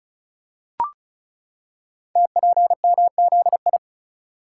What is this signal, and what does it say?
Morse code, heard as a keyed tone, about 36 wpm, 690 Hz. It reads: TPMZI
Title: Code Groups - 5 Letters - 35wpm